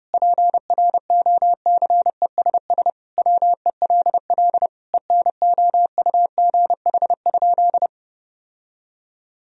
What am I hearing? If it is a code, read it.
Morse code, heard as a keyed tone, about 30 wpm, 700 Hz. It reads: PROCESS WELL ENOUGH?
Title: Sentences from Top 1000 Words - Review Part 3 - 30wpm